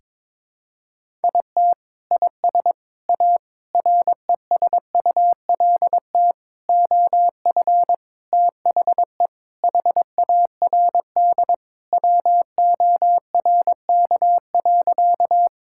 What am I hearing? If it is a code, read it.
Morse code, heard as a keyed tone, about 22 wpm, 700 Hz. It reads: IT IS A RESULT OF THE HARD WORK.
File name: practice / 22wpm / Sentences with 4-Word Phrases - 22wpm